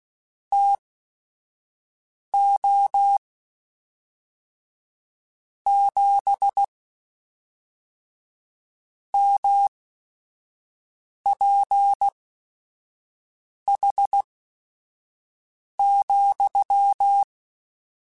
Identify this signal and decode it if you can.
Morse code, heard as a keyed tone, about 16 wpm, spaced as for 5 wpm, 800 Hz. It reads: TO 7 MPH,
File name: Weather Sets / Weather Sets-mp3s / Weather 4 24 @5